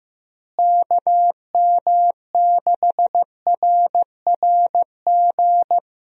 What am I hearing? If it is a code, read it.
Morse code, heard as a keyed tone, about 15 wpm, 700 Hz. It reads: KM6RRG
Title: USA Call Signs - 6 Characters - Repeated 2x - 15wpm